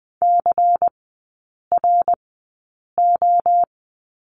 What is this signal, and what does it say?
Morse code, heard as a keyed tone, about 20 wpm, spaced as for 8 wpm, 690 Hz. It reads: CRO
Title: 093 Lesson 16e - Letter c - Three Characters 20wpm